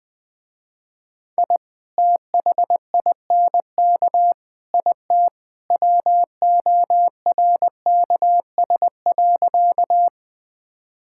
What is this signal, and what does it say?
Morse code, heard as a keyed tone, about 20 wpm, 700 Hz. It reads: I THINK IT WORKS.
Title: Sentences with 2-Word Phrases - Spoken Beginning - 20wpm